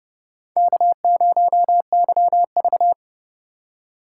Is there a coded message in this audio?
Yes: Morse code from 0.6 to 2.9 s, heard as a keyed tone, about 29 wpm, 700 Hz. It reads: K0YV